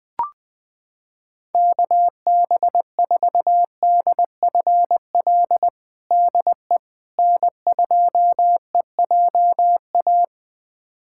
Morse code, heard as a keyed tone, about 21 wpm, 700 Hz. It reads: KB4DFL DE N2EJA
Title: USA Callsign DE Callsign 20wpm